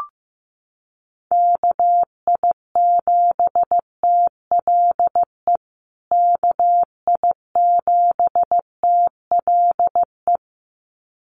Morse code, heard as a keyed tone, about 15 wpm, 700 Hz. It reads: KI7TLE KI7TLE
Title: USA Call Signs Repeated 2x 15wpm